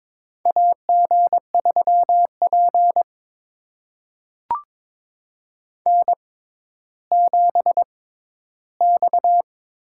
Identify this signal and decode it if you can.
Morse code, heard as a keyed tone, about 22 wpm, spaced as for 13 wpm, 700 Hz. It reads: AG3P N7X
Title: USA Call Signs - 4 Characters - 6x Character Spacing - 22wpm